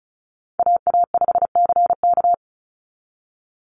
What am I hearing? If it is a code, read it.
Morse code, heard as a keyed tone, about 34 wpm, 700 Hz. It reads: AA5CK
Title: NA SOTA Chaser Callsigns 35wpm